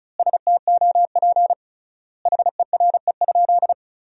Morse code, heard as a keyed tone, about 36 wpm, 700 Hz. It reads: STOP HERE?
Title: 3x Word-Spacing - Sentences from Top 300 Words - 35wpm